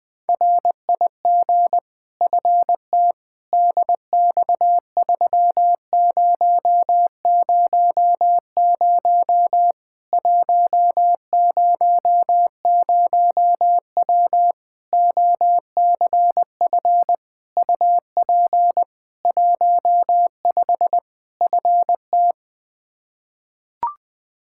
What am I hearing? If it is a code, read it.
Morse code, heard as a keyed tone, about 20 wpm, 700 Hz. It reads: RIG FT DX3000 100W OCF UP 15 FT
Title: K1USN SST QSO Practice 20wpm